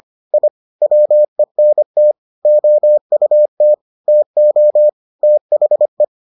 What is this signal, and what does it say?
Morse code, heard as a keyed tone, about 25 wpm, 600 Hz. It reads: I WENT OUT TO THE